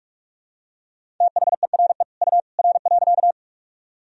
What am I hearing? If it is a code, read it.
Morse code, heard as a keyed tone, about 43 wpm, 700 Hz. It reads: THERE U R.
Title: Sentences from Top 100 Words and Abbreviations 45wpm